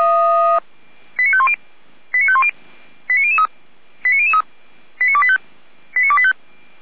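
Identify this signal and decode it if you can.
Five-tone selective call (ZVEI), heard as quick runs of five tones, tones of about 70 ms. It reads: 89510 89510 89003 89003 89285 89285
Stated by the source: Received on SDR, ZVEI five-tone selective call.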